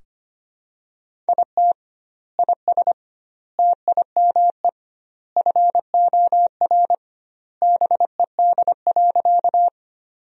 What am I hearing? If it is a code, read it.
Morse code, heard as a keyed tone, about 25 wpm, 700 Hz. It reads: IT IS TIME FOR BED.
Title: 2x Word-Spacing - Sentences from Top 500 Words - 25wpm